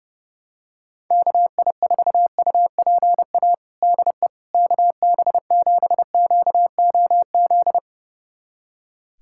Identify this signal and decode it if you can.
Morse code, heard as a keyed tone, about 30 wpm, 700 Hz. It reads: KI4UPA DE KB7QOZ
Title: USA Callsign DE Callsign 30wpm